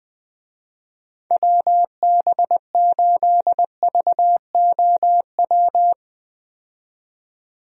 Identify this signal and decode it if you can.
Morse code, heard as a keyed tone, about 20 wpm, 700 Hz. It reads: WB8VOW
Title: USA Call Signs - 6 Characters - 20wpm